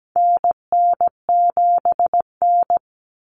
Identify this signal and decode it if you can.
Morse code, heard as a keyed tone, about 17 wpm, 700 Hz. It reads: NN7N
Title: Common Cut Numbers - 4-Digit 17wpm